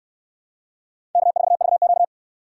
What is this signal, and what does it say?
Morse code, heard as a keyed tone, about 65 wpm, 700 Hz. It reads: K4VQ